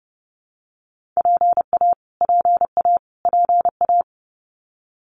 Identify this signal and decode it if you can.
Morse code, heard as a keyed tone, about 30 wpm, 700 Hz. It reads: PA PA PA
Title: US States Repeated 3x 30wpm